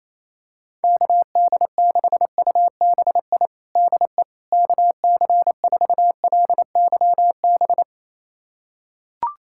Morse code, heard as a keyed tone, about 28 wpm, 700 Hz. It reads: KD6UBI DE KC4LYB
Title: USA Callsign DE Callsign - 6 Characters - 28wpm